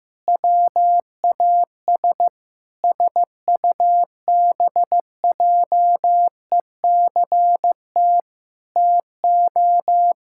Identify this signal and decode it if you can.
Morse code, heard as a keyed tone, about 15 wpm, 700 Hz. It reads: WAS SUBJECT TO
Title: Pangram Challenge v2 - 15wpm